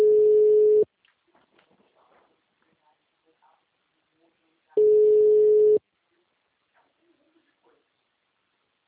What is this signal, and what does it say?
Ringback tone (Europe, 425 Hz), with speech.
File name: out- -2011- - - .40128